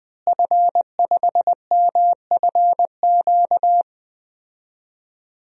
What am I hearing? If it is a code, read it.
Morse code, heard as a keyed tone, about 21 wpm, 700 Hz. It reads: F5MFQ
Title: DXCC-100 Call Signs - 5 Characters - Repeated 2x - 20wpm